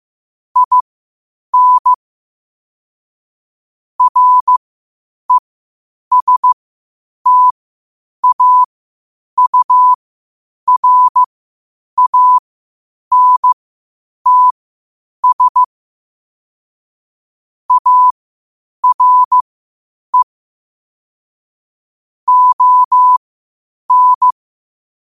Morse code, heard as a keyed tone, about 15 wpm, spaced as for 8 wpm, 1000 Hz. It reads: IN RESTAURANTS ARE ON
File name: qotdfiles / 8wpm